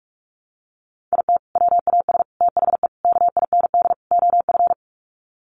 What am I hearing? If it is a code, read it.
Morse code, heard as a keyed tone, about 44 wpm, 700 Hz. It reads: IT WAS THE KIND OF
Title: Sets of 5 Words Encore2 45wpm